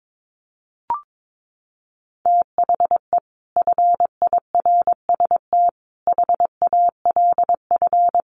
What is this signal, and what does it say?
Morse code, heard as a keyed tone, about 22 wpm, 700 Hz. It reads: THE FIRST HALF
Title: Sets of 5 Words 22wpm